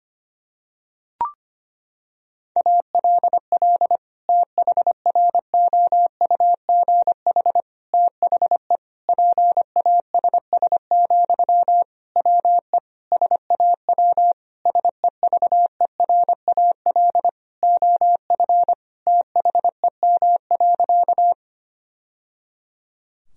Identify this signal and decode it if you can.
Morse code, heard as a keyed tone, about 25 wpm, 700 Hz. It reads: ALL THROUGH THE PASS, WE SAW SEVERAL OF THEM.
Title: Sentences from Top 1000 Words - Review Part 2 - 25wpm